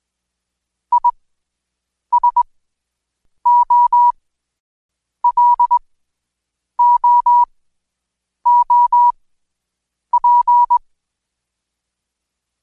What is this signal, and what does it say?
Morse code, heard as a keyed tone, about 21 wpm, spaced as for 7 wpm, 970 Hz. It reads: ISOLOOP
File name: Code Quick / AudioDownld2 / Seg112-Day17-DayDrill 17